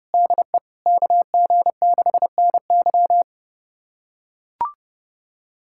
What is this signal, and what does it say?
Morse code, heard as a keyed tone, about 30 wpm, 700 Hz. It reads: DE KG6NY
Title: USA Callsign DE Callsign - 5 Characters - 30wpm